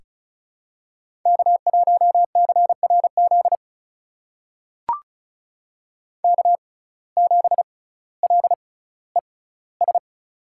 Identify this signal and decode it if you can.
Morse code, heard as a keyed tone, about 36 wpm, spaced as for 18 wpm, 700 Hz. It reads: K1CRZ K7LES
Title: USA Call Signs - 6x Character Spacing - 35wpm